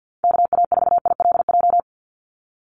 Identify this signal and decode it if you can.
Morse code, heard as a keyed tone, about 49 wpm, 700 Hz. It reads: KA4ILJ
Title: USA Call Signs 50wpm